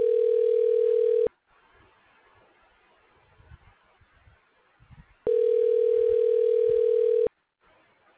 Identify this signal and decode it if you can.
Ringback tone (North America), 440 + 480 Hz.